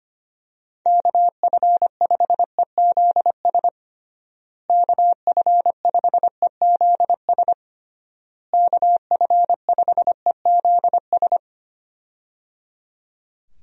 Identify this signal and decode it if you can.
Morse code, heard as a keyed tone, about 25 wpm, 700 Hz. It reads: KF5EZS KF5EZS KF5EZS
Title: USA Call Signs - 6 Characters - Repeated 3x - 25wpm